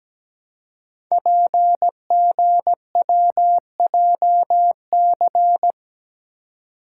Morse code, heard as a keyed tone, about 17 wpm, 700 Hz. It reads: PGWJC
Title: Code Groups - 5 Letters - 17wpm